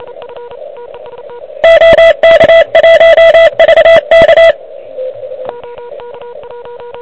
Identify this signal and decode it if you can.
Morse code, heard as a keyed tone, about 28 wpm, 660 Hz, with radio noise. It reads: OK1VK
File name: ARCHIV / QCX / QCX Pileup